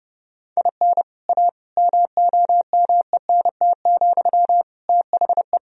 Morse code, heard as a keyed tone, about 30 wpm, 700 Hz. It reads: IN A MOMENT, THE
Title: Sentences from Top 1000 Words - Review Part 3 - 30wpm